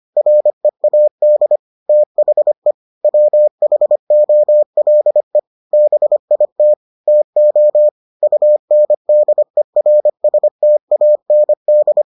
Morse code, heard as a keyed tone, about 25 wpm, 600 Hz. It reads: READ THE WHOLE BIT TO UNDERSTAND